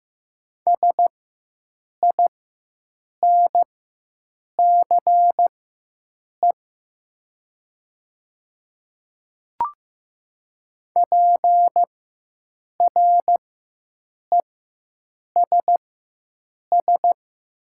Morse code, heard as a keyed tone, about 15 wpm, spaced as for 7 wpm, 700 Hz. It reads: SINCE PRESS
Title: 4x Character Spacing - Top 300 Words - 15wpm